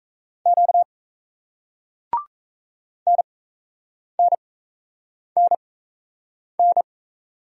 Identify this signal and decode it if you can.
Morse code, heard as a keyed tone, about 38 wpm, spaced as for 8 wpm, 700 Hz. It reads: Q NNNN